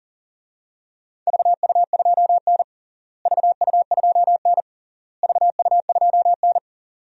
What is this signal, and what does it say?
Morse code, heard as a keyed tone, about 40 wpm, 700 Hz. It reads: VU2N VU2N VU2N
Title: DXCC-100 Call Signs - 4 Characters - Repeated 3x - 40wpm